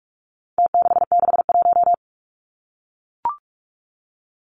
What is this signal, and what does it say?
Morse code, heard as a keyed tone, about 44 wpm, 700 Hz. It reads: T661
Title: Numbers - Contest Style - 4-Digit 45wpm